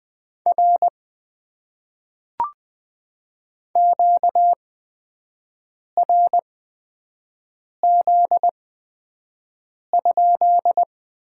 Morse code, heard as a keyed tone, about 20 wpm, spaced as for 5 wpm, 700 Hz. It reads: R QRZ?
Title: Top 90 QSO Elements - 8x Character Spacing - 20wpm